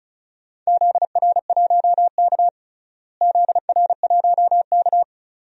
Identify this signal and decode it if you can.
Morse code, heard as a keyed tone, about 36 wpm, 700 Hz. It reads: ZR1K ZR1K
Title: DXCC-100 Call Signs - 4 Characters - Repeated 2x - 35wpm